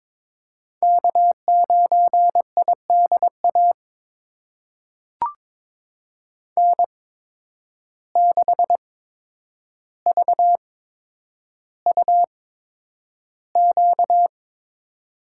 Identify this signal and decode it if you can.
Morse code, heard as a keyed tone, about 22 wpm, spaced as for 9 wpm, 700 Hz. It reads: K9IDA N6VUQ
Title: USA Call Signs - 5 Characters - 8x Character Spacing - 22wpm